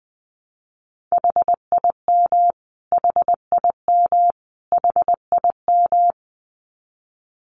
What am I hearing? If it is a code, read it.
Morse code, heard as a keyed tone, about 20 wpm, 700 Hz. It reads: HIM HIM HIM
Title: Top 200 Words Repeated 20wpm